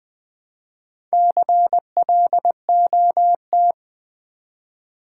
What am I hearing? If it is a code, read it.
Morse code, heard as a keyed tone, about 20 wpm, 700 Hz. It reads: CLOT
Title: Build A Word - 1 Letter At A Time - 5-Letter Words - 20wpm